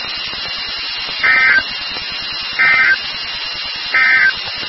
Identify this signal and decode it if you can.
Emergency Alert System (SAME) tones, heard as harsh buzzing bursts. It is heard from 1.1 to 4.4 s.
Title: NOAA Weather Radio WXK36